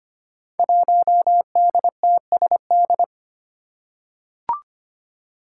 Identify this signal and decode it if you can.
Morse code, heard as a keyed tone, about 25 wpm, 700 Hz. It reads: 1DTSD
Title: Five Challenging Letter-Number Combinations 25wpm